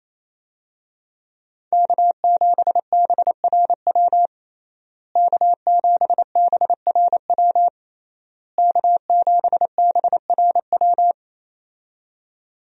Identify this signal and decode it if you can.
Morse code, heard as a keyed tone, about 28 wpm, 700 Hz. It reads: K7BRW K7BRW K7BRW
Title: USA Call Signs - 5 Characters - Repeated 3x - 28wpm